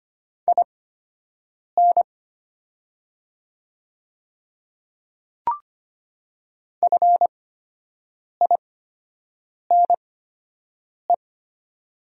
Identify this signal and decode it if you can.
Morse code, heard as a keyed tone, about 25 wpm, spaced as for 7 wpm, 700 Hz. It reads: IN FINE
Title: 8x Character Spacing - Top 500 Words - 25wpm